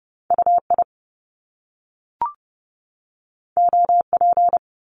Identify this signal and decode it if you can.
Morse code, heard as a keyed tone, about 30 wpm, 700 Hz. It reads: UI OP
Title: Top Two Letter Combinations in Words 30wpm